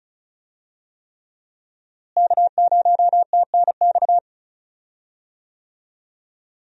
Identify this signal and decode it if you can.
Morse code, heard as a keyed tone, about 36 wpm, 700 Hz. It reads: K0TNX
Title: USA Call Signs - 5 Characters - Repeated 3x - 35wpm